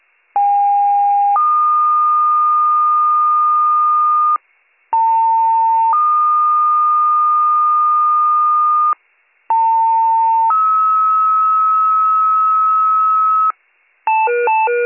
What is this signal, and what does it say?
Two-tone page, heard as a one-second tone, then a three-second tone, about 810 then 1220 Hz, about 900 then 1220 Hz, about 900 then 1290 Hz. It is heard from 0.3 to 13.5 s.